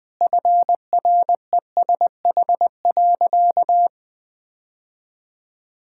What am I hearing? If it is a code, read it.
Morse code, heard as a keyed tone, about 20 wpm, 700 Hz. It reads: FRESH.
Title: Sentences with 2-Word Phrases - 20wpm